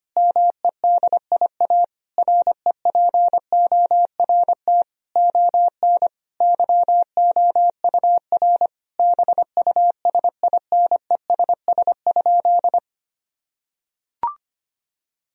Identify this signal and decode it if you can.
Morse code, heard as a keyed tone, about 25 wpm, 700 Hz. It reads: MEDIA REPORT ON YOUR BUSINESS?
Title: Sentences from Top 1000 Words - Review Part 4 - 25wpm